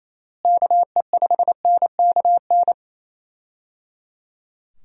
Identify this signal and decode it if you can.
Morse code, heard as a keyed tone, about 27 wpm, 700 Hz. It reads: KE5NKN
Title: USA Call Signs - 6 Characters - Repeated 3x - 28wpm